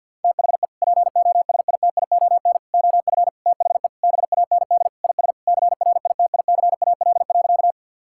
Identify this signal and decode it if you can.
Morse code, heard as a keyed tone, about 50 wpm, 700 Hz. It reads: THE POSITION OF THE BAND IS CRITICAL.
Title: Sentences from Top 1600 Words 50wpm